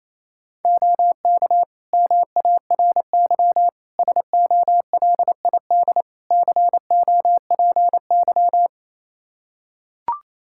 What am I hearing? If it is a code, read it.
Morse code, heard as a keyed tone, about 28 wpm, 700 Hz. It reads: OK MARY SOLID COPY
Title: Practice QSO - Exchange 3 - More Info - 28wpm